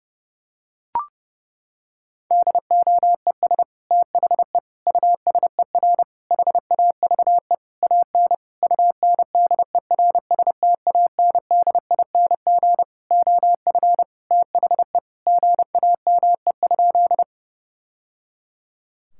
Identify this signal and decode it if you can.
Morse code, heard as a keyed tone, about 30 wpm, 700 Hz. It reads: DOES THE USER HAVE AN UNDERSTANDING OF THE GAME?
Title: Sentences from Top 1000 Words - Review Part 5 - 30wpm